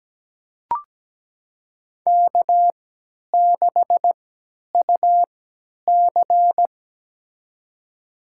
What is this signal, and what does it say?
Morse code, heard as a keyed tone, about 17 wpm, spaced as for 10 wpm, 700 Hz. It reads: K6UC
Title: USA Call Signs - 4 Characters - 3x Character Spacing - 17wpm